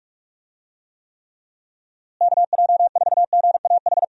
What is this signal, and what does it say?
Morse code, heard as a keyed tone, about 46 wpm, 700 Hz. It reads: KJ4GAH
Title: USA Call Signs - 6 Characters - Repeated 3x - 45wpm